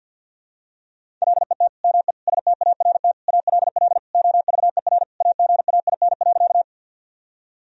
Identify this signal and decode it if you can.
Morse code, heard as a keyed tone, about 51 wpm, 700 Hz. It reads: LET ME START ALL OVER AGAIN.
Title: Sentences from Top 1000 Words - Review Part 1 - 50wpm